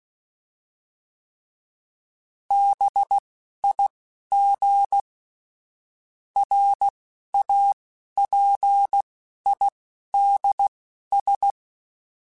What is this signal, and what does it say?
Morse code, heard as a keyed tone, about 16 wpm, spaced as for 11 wpm, 800 Hz. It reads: BIG RAPIDS
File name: fromSMs / FromSMs mp3s / Cities@10